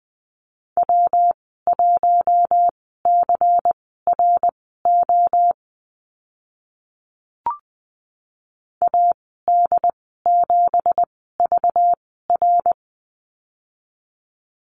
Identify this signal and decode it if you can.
Morse code, heard as a keyed tone, about 20 wpm, spaced as for 15 wpm, 700 Hz. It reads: W1CRO AD7VR
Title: 193 Lesson 31g - Letter v - Call Signs 20wpm